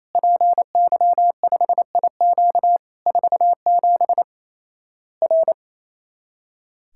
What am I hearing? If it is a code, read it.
Morse code, heard as a keyed tone, about 28 wpm, 700 Hz. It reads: PY5IQ 47 R
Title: Worked All Europe DX Contest - QTC Exchange - 28wpm